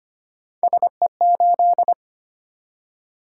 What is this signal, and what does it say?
Morse code, heard as a keyed tone, about 25 wpm, 700 Hz. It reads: SE8
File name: Three Letter-Number Combinations - 25wpm